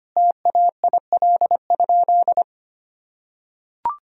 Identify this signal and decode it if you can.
Morse code, heard as a keyed tone, about 25 wpm, 700 Hz. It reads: TAIL?
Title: Sentences from Top 1000 Words - Review Part 3 - 25wpm